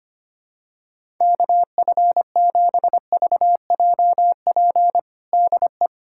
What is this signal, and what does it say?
Morse code, heard as a keyed tone, about 25 wpm, 700 Hz. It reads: KF7VJP DE
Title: USA Callsign DE Callsign 25wpm